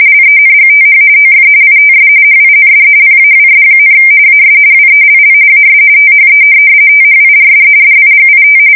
RTTY, about 45.45 baud. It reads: LENGTH OF ITS HIND PAWS, AS WELL AS THE THIRD DIGIT
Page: 170 Shift 2125/2295